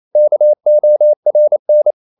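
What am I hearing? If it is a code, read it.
Morse code, heard as a keyed tone, about 28 wpm, 600 Hz. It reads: KORN
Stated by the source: Danish short words sent in Morse at 28 wpm, from CW med Gnister.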